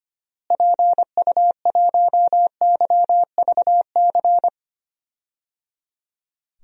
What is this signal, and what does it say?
Morse code, heard as a keyed tone, about 25 wpm, 700 Hz. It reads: PU1YVC
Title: DXCC-100 Call Signs - 6 Characters - Repeated 3x - 25wpm